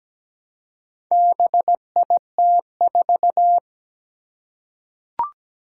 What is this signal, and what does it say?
Morse code, heard as a keyed tone, about 17 wpm, 700 Hz. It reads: BIT4
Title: Four Challenging Letter-Number Combinations 17wpm